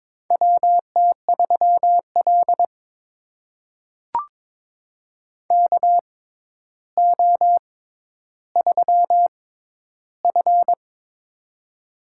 Morse code, heard as a keyed tone, about 22 wpm, spaced as for 11 wpm, 700 Hz. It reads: WT3L KO3F
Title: USA Call Signs - 4 Characters - 6x Character Spacing - 22wpm